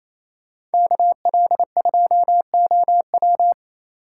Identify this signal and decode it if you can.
Morse code, heard as a keyed tone, about 34 wpm, 700 Hz. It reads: KL2OW
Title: Speed Racing - USA Call Signs 28wpm